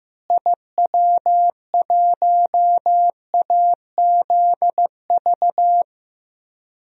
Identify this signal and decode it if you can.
Morse code, heard as a keyed tone, about 15 wpm, 700 Hz. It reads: IW1AZV
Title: DXCC-100 Call Signs - 6 Characters - Repeated 3x - 15wpm